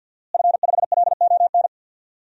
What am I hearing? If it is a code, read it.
Morse code, heard as a keyed tone, about 52 wpm, 700 Hz. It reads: F5LON
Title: DXCC-100 Call Signs - 5 Characters - Repeated 2x - 50wpm